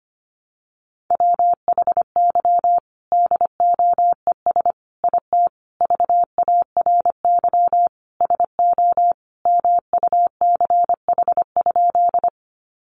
Morse code, heard as a keyed tone, about 25 wpm, 700 Hz. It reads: WHY DOES IT VARY SO MUCH?
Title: Sentences from Top 1200 Words 25wpm